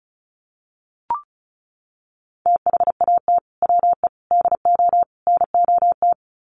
Morse code, heard as a keyed tone, about 34 wpm, 700 Hz. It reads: THAT WE DO NOT
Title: Sets of 4 Words 35wpm